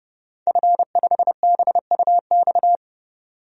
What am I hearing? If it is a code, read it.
Morse code, heard as a keyed tone, about 29 wpm, 700 Hz. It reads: F5BUX
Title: DXCC-100 Call Signs - 5 Characters - 30wpm